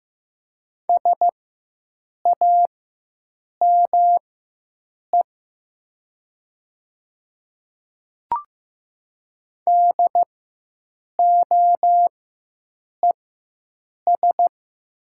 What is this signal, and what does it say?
Morse code, heard as a keyed tone, about 15 wpm, spaced as for 7 wpm, 700 Hz. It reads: SAME DOES
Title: 4x Character Spacing - Top 200 Words - 15wpm